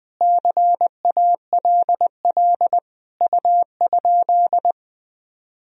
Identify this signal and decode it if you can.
Morse code, heard as a keyed tone, about 20 wpm, 700 Hz. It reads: CALL U?
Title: Sentences from Top 100 Words and Abbreviations 20wpm